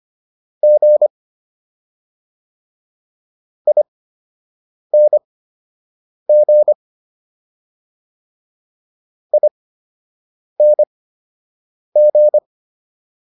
Morse code, heard as a keyed tone, about 25 wpm, spaced as for 7 wpm, 600 Hz. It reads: G ING ING